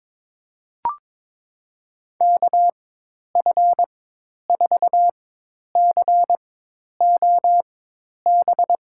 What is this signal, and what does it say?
Morse code, heard as a keyed tone, about 22 wpm, spaced as for 10 wpm, 700 Hz. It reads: KF4COB
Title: USA Call Signs - 6 Characters - 4x Character Spacing - 22wpm